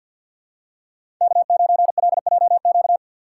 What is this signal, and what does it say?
Morse code, heard as a keyed tone, about 48 wpm, 700 Hz. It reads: K9LJQ
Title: USA Call Signs - 6x Character Spacing - 50wpm